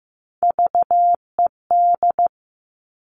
Morse code, heard as a keyed tone, about 15 wpm, 700 Hz. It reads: VED
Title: All Cut Numbers - 3-Digit 15wpm